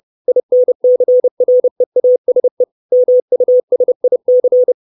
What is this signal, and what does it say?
Morse code, heard as a keyed tone, about 30 wpm, 500 Hz. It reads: INCREASE MUSIC
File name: 632 Words 30WPM